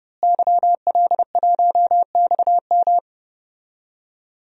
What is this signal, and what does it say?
Morse code, heard as a keyed tone, about 30 wpm, 700 Hz. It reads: YL1XM
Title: DXCC-100 Call Signs - 5 Characters - Repeated 2x - 30wpm